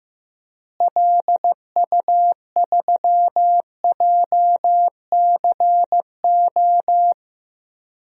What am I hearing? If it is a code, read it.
Morse code, heard as a keyed tone, about 14 wpm, 700 Hz. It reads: LU3JCO
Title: DXCC-100 Call Signs - 6 Characters - Repeated 3x - 15wpm